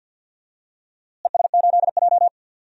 Morse code, heard as a keyed tone, about 49 wpm, 700 Hz. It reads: ES8J
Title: DXCC-100 Call Signs - 4 Characters - Repeated 2x - 50wpm